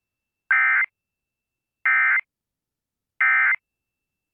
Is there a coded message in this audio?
Yes: Emergency Alert System (SAME) tones from 0.5 to 3.6 s, heard as harsh buzzing bursts.